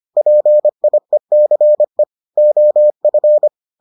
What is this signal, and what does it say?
Morse code, heard as a keyed tone, about 25 wpm, 600 Hz. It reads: PIECE OF